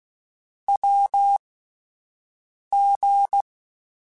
Morse code, heard as a keyed tone, about 16 wpm, spaced as for 5 wpm, 800 Hz. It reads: WG
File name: Short Runs / Short Runs-mp3s / Unballanced @10